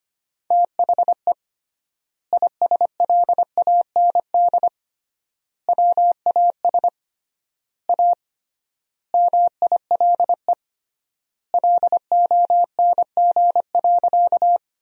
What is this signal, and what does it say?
Morse code, heard as a keyed tone, about 25 wpm, 700 Hz. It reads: THE ISLAND WAS A MILE LONG.
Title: 3x Word-Spacing - Sentences from Top 500 Words - 25wpm